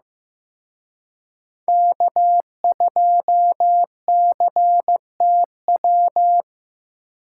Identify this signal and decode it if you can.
Morse code, heard as a keyed tone, about 18 wpm, 700 Hz. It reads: K2CTW
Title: Speed Racing - USA 1x3 Call Signs 15wpm